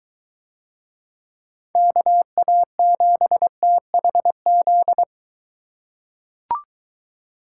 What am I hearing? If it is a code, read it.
Morse code, heard as a keyed tone, about 22 wpm, 700 Hz. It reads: KA7THZ